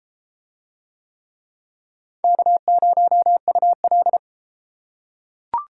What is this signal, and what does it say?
Morse code, heard as a keyed tone, about 31 wpm, 700 Hz. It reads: K0UL